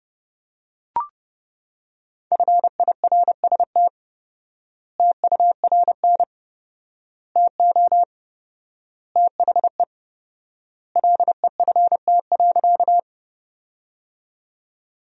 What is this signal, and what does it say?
Morse code, heard as a keyed tone, about 30 wpm, 700 Hz. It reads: FIRST TURN TO THE LEFT.
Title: 4x Word-Spacing - Sentences from Top 300 Words - 30wpm